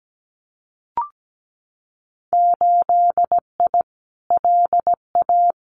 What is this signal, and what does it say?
Morse code, heard as a keyed tone, about 17 wpm, 700 Hz. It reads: 8I LA
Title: Winter Field Day Exchanges Encore 17wpm